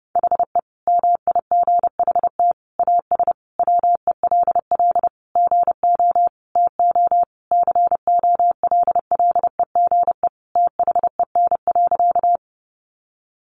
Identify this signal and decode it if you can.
Morse code, heard as a keyed tone, about 30 wpm, 700 Hz. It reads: HE MIGHT AS WELL GO TO COLLEGE THEN.
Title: Sentences from Top 1300 Words 30wpm